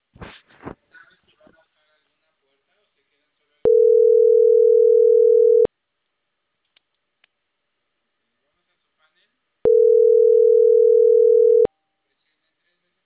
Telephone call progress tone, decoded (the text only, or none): ringback tone